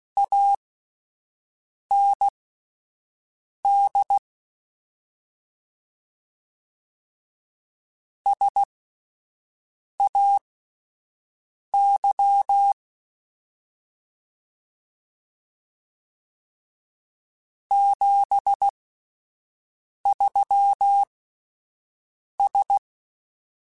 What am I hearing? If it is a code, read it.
Morse code, heard as a keyed tone, about 16 wpm, spaced as for 5 wpm, 800 Hz. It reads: AND SAY 73S
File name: QSOs Spaced